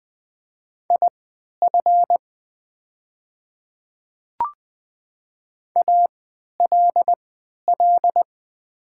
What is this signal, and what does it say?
Morse code, heard as a keyed tone, about 20 wpm, spaced as for 11 wpm, 700 Hz. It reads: IF ALL
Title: 3x Character Spacing - Top 100 Words - 20wpm